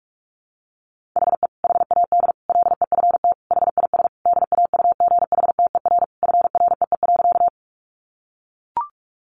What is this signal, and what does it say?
Morse code, heard as a keyed tone, about 44 wpm, 700 Hz. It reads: HE HAD LEFT HIS DAUGHTER FREE.
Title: Sentences from Top 1500 Words 45wpm